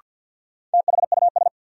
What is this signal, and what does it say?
Morse code, heard as a keyed tone, about 46 wpm, 700 Hz. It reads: THUS
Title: Sentences from Top 800 Words - New Word Drill - 50wpm